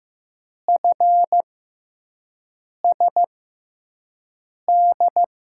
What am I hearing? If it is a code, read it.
Morse code, heard as a keyed tone, about 15 wpm, spaced as for 5 wpm, 700 Hz. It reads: FSD
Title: USA Call Signs - 6 Characters - 6x Character Spacing - 15wpm